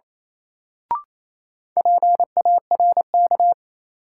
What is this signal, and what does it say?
Morse code, heard as a keyed tone, about 28 wpm, 700 Hz. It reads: PARK
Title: Sentences from Top 1300 Words - New Word Drill - 28wpm